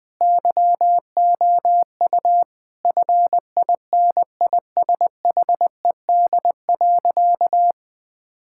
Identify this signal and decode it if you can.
Morse code, heard as a keyed tone, about 20 wpm, 700 Hz. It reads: YOU FINISHED.
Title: Sentences with 2-Word Phrases - 20wpm